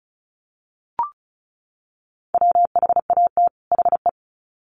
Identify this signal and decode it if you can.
Morse code, heard as a keyed tone, about 34 wpm, 700 Hz. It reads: WHAT HE
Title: Sets of 2 Words 35wpm